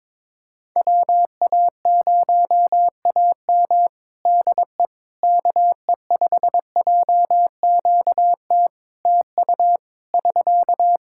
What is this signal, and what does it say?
Morse code, heard as a keyed tone, about 22 wpm, 700 Hz. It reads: WA0AM DE KE5JQT TU <SK>
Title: Practice QSO - Exchange 4 - Ending - 22wpm